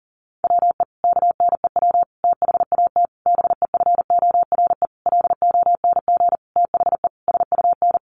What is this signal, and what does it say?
Morse code, heard as a keyed tone, about 40 wpm, 700 Hz. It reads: WE KNEW THAT BEFORE LONG THE SUN
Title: Sentences from Top 400 Words 40wpm